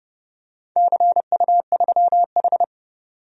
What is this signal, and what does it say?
Morse code, heard as a keyed tone, about 28 wpm, 700 Hz. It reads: CU3H
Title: DXCC-100 Call Signs - 4 Characters - 30wpm